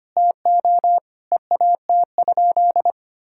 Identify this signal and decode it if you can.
Morse code, heard as a keyed tone, about 25 wpm, 700 Hz. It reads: TO EAT?
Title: Sentences from Top 1600 Words 25wpm